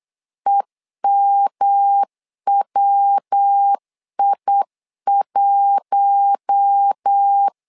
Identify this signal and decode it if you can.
Morse code, heard as a keyed tone, about 8 wpm, 800 Hz. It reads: EMWI1
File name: LEZIONI / 4 Q1 8wpm 3035